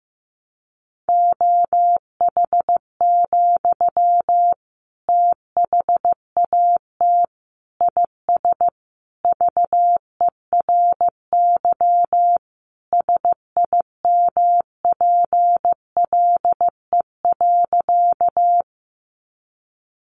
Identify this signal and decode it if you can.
Morse code, heard as a keyed tone, about 15 wpm, 700 Hz. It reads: OH, THAT IS VERY SIMPLE.
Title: Sentences from Top 500 Words 15wpm